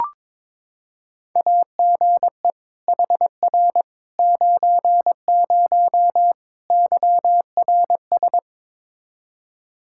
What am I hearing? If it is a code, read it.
Morse code, heard as a keyed tone, about 22 wpm, 700 Hz. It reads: AGE HR 90 YRS
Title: Practice QSO - Exchange 3 - More Info - 22wpm